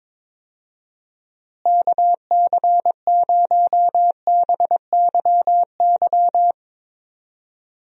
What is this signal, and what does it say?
Morse code, heard as a keyed tone, about 21 wpm, 700 Hz. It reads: KC0BYY
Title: USA Call Signs - 6 Characters - Repeated 2x - 22wpm